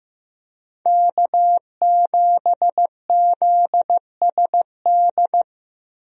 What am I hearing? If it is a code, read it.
Morse code, heard as a keyed tone, about 15 wpm, 700 Hz. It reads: K7ZSD
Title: USA Call Signs - 5 Characters - 15wpm